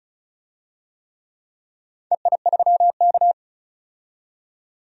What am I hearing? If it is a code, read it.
Morse code, heard as a keyed tone, about 34 wpm, 700 Hz. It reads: EI3K
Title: DXCC-100 Call Signs - 4 Characters - 35wpm